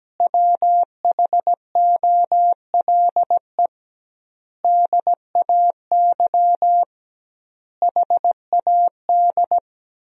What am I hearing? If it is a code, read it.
Morse code, heard as a keyed tone, about 17 wpm, 700 Hz. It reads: WHOLE DAY HAD
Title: 2x Word-Spacing - Sentences from Top 500 Words - 17wpm